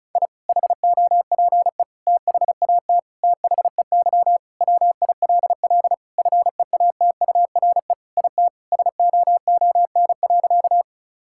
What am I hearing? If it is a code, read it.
Morse code, heard as a keyed tone, about 34 wpm, 700 Hz. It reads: I HOPE THAT THEY WILL FEATURE IT SOON.
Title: Sentences from Top 1000 Words - Review Part 5 - 35wpm